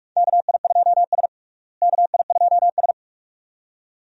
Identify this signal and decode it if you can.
Morse code, heard as a keyed tone, about 46 wpm, 700 Hz. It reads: KI2S KI2S
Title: USA Call Signs - 4 Characters - Repeated 2x - 45wpm